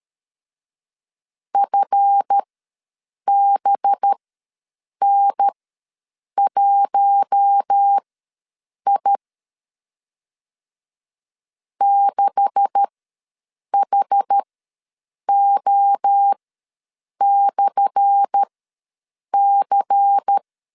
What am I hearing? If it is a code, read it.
Morse code, heard as a keyed tone, about 12 wpm, spaced as for 7 wpm, 800 Hz. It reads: FBN1I 6HO/C
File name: LEZIONI / 10 SIGNS / 10A SLASH ws 12wpm 3443